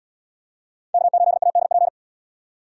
Morse code, heard as a keyed tone, about 77 wpm, 700 Hz. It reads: K8ADP